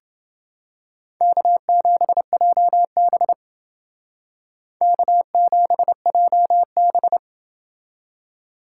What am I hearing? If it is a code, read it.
Morse code, heard as a keyed tone, about 29 wpm, 700 Hz. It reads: K7JB K7JB